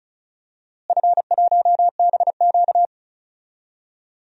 Morse code, heard as a keyed tone, about 34 wpm, 700 Hz. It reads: F1BQ
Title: DXCC-100 Call Signs - 4 Characters - Repeated 2x - 35wpm